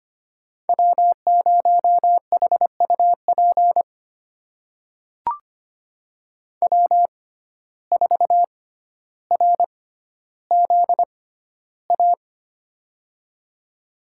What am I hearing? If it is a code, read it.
Morse code, heard as a keyed tone, about 25 wpm, spaced as for 13 wpm, 700 Hz. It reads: W0HUP W4RZA
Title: USA Call Signs - 5 Characters - 6x Character Spacing - 25wpm